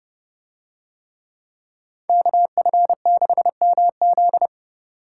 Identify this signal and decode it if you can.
Morse code, heard as a keyed tone, about 28 wpm, 700 Hz. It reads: KF6MZ